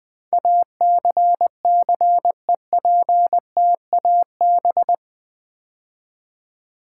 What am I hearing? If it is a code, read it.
Morse code, heard as a keyed tone, about 20 wpm, 700 Hz. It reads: ACCEPTAB
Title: Build A Word - 4 Letters At A Time - 10-Letter Words - 20wpm